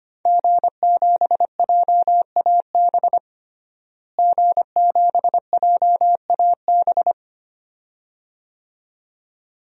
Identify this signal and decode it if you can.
Morse code, heard as a keyed tone, about 25 wpm, 700 Hz. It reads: G7JAB G7JAB
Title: DXCC-100 Call Signs - 5 Characters - Repeated 2x - 25wpm